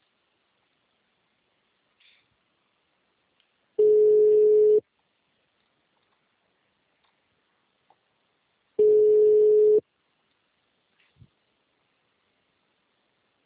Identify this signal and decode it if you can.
Ringback tone (Europe, 425 Hz).
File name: out- -2032- - - .55919